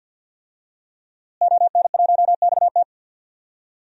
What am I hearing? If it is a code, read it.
Morse code, heard as a keyed tone, about 46 wpm, 700 Hz. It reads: ON1XT
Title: DXCC-100 Call Signs - 5 Characters - Repeated 2x - 50wpm